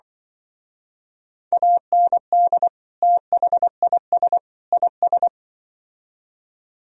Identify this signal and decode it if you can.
Morse code, heard as a keyed tone, about 25 wpm, 700 Hz. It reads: AND THIS IS
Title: Speed-Racing - Sets of 3 Words 20wpm